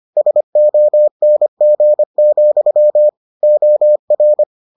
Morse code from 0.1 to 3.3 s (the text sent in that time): SONG,